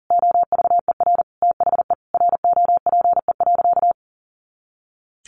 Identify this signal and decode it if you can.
Morse code, heard as a keyed tone, about 40 wpm, 700 Hz. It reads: OVER THE ROPE.
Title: Easy Phrases 40wpm